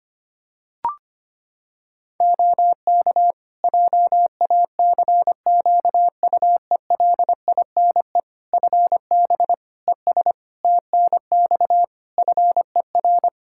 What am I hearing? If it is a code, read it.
Morse code, heard as a keyed tone, about 25 wpm, 700 Hz. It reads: OK JACQUELINE FB ES TNX FER
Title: K1USN SST QSO Practice 25wpm